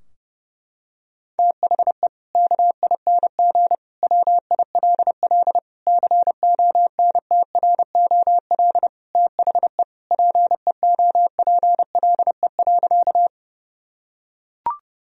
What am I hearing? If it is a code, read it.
Morse code, heard as a keyed tone, about 30 wpm, 700 Hz. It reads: THE KING WILL CONTROL THE PEOPLE.
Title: Sentences from Top 1000 Words - Review Part 3 - 30wpm